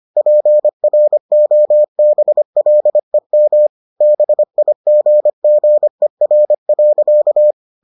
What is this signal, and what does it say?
Morse code, heard as a keyed tone, about 25 wpm, 600 Hz. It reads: PROBLEM BIGGER.